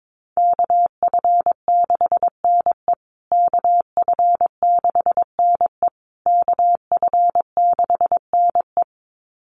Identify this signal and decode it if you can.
Morse code, heard as a keyed tone, about 22 wpm, 700 Hz. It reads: KF6NE KF6NE KF6NE
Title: USA Call Signs Repeated 3x 22wpm